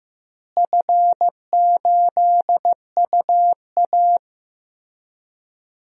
Morse code, heard as a keyed tone, about 15 wpm, 700 Hz. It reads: F8UA
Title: DXCC-100 Call Signs - 4 Characters - Repeated 3x - 15wpm